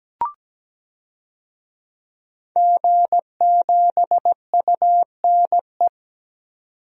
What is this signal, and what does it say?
Morse code, heard as a keyed tone, about 17 wpm, 700 Hz. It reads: G7UNE
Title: DXCC-100 Call Signs - 5 Characters - 17wpm